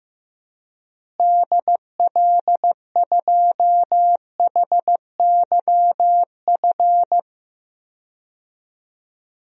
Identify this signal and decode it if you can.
Morse code, heard as a keyed tone, about 15 wpm, 700 Hz. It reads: DL2HYF
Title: DXCC-100 Call Signs - 6 Characters - Repeated 2x - 15wpm